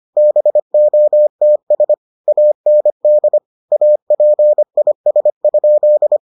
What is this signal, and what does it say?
Morse code, heard as a keyed tone, about 25 wpm, 600 Hz. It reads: BOTS AND APIS?